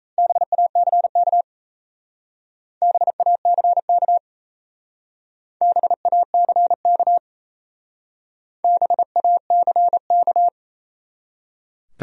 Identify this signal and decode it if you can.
Morse code, heard as a keyed tone, about 36 wpm, 700 Hz. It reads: BACK BACK BACK BACK